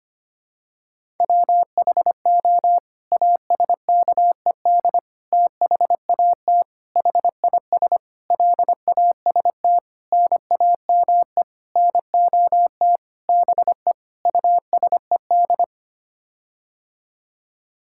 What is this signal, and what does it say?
Morse code, heard as a keyed tone, about 25 wpm, 700 Hz. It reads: WHO ASKED THAT HIS LAST NAME NOT BE USED
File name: Sets of 9 Words - 25wpm